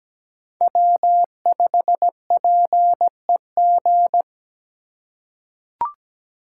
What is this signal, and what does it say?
Morse code, heard as a keyed tone, about 17 wpm, 700 Hz. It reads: W5PEG
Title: USA Call Signs - 3x Character Spacing - 17wpm